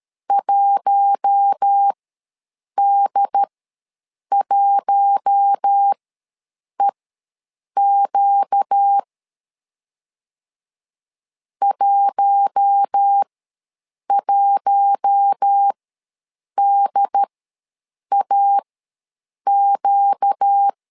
Morse code, heard as a keyed tone, about 13 wpm, spaced as for 7 wpm, 800 Hz. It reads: 1D1EQ 11DAQ
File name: LEZIONI / 4 Q1 ws 12wpm 2129